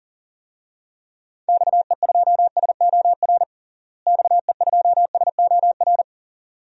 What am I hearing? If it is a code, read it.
Morse code, heard as a keyed tone, about 40 wpm, 700 Hz. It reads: XE2SOR XE2SOR
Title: DXCC-100 Call Signs - 6 Characters - Repeated 2x - 40wpm